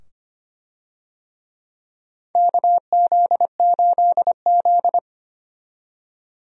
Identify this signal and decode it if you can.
Morse code, heard as a keyed tone, about 25 wpm, 700 Hz. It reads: KZ8Z
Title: USA Call Signs - 4 Characters - 25wpm